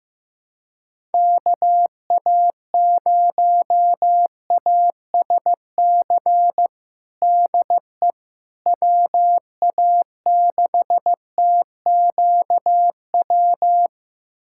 Morse code, heard as a keyed tone, about 15 wpm, 700 Hz. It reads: KA0ASC DE WA6TQW
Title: USA Callsign DE Callsign - 6 Characters - 15wpm